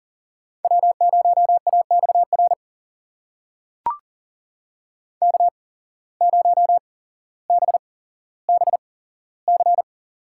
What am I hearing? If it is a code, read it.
Morse code, heard as a keyed tone, about 37 wpm, spaced as for 17 wpm, 700 Hz. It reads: W0AXR K0BBC
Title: USA Call Signs - 5 Characters - 8x Character Spacing - 40wpm